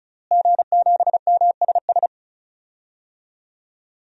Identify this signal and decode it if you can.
Morse code, heard as a keyed tone, about 36 wpm, 700 Hz. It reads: G7MSS
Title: DXCC-100 Call Signs - 5 Characters - Repeated 2x - 35wpm